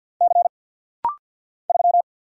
Morse code, heard as a keyed tone, about 50 wpm, 690 Hz. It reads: C 3
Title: Single Letter-Number Alt - Rapid-Fire 50wpm